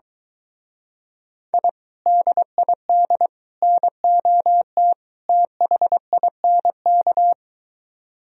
Morse code, heard as a keyed tone, about 23 wpm, 700 Hz. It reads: I DID NOT THINK